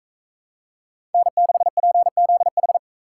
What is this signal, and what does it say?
Morse code, heard as a keyed tone, about 41 wpm, 700 Hz. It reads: N6PZH